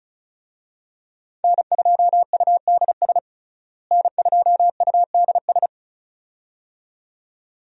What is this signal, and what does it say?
Morse code, heard as a keyed tone, about 36 wpm, 700 Hz. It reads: N2UDS N2UDS
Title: USA Call Signs - 5 Characters - Repeated 2x - 35wpm